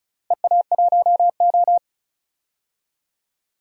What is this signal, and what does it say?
Morse code, heard as a keyed tone, about 36 wpm, 700 Hz. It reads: EA1O
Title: DXCC-100 Call Signs - 4 Characters - Repeated 2x - 35wpm